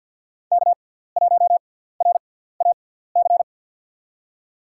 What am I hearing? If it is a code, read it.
Morse code, heard as a keyed tone, about 52 wpm, spaced as for 17 wpm, 700 Hz. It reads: K1RAC
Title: USA Call Signs - 5 Characters - 6x Character Spacing - 50wpm